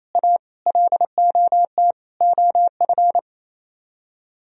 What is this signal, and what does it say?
Morse code, heard as a keyed tone, about 36 wpm, 700 Hz. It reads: A LOT OF